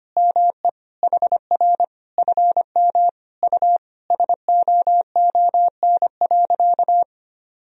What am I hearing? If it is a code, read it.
Morse code, heard as a keyed tone, about 25 wpm, 700 Hz. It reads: ME HR FM U SOON.
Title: Sentences from Top 400 Words and Abbreviations 25wpm